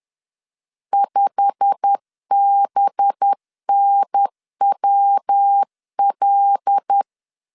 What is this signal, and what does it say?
Morse code, heard as a keyed tone, about 10 wpm, 800 Hz. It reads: 5BNWL